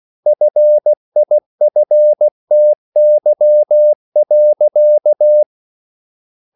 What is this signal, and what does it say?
Morse code, heard as a keyed tone, about 16 wpm, 600 Hz. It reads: FIFTY.